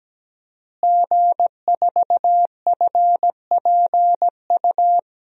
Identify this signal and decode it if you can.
Morse code, heard as a keyed tone, about 17 wpm, 700 Hz. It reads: G4FPU
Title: DXCC-100 Call Signs - 5 Characters - Repeated 2x - 17wpm